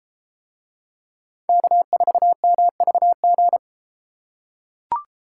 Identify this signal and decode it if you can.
Morse code, heard as a keyed tone, about 33 wpm, 700 Hz. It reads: K4MVG